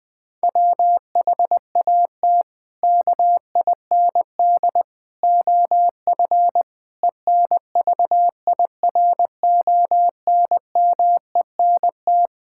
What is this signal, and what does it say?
Morse code, heard as a keyed tone, about 20 wpm, 700 Hz. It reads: WHAT KIND OF ENVIRONMENT
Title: Sentences from Top 1000 Words - Review Part 5 - 20wpm